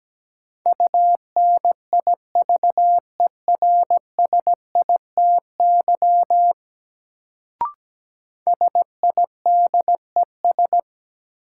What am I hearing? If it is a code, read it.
Morse code, heard as a keyed tone, about 17 wpm, 700 Hz. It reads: UNIVERSITY SIDES
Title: Sets of 4 Words - New Word Drill - 17wpm